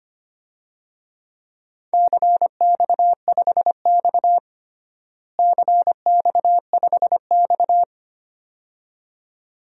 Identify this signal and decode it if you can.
Morse code, heard as a keyed tone, about 25 wpm, 700 Hz. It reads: CX5X CX5X
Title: DXCC-100 Call Signs - 4 Characters - Repeated 2x - 25wpm